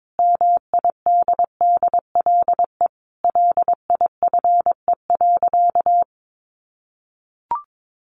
Morse code, heard as a keyed tone, about 22 wpm, 700 Hz. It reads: MIDDLE LIFE.
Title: Sentences from Top 1300 Words 22wpm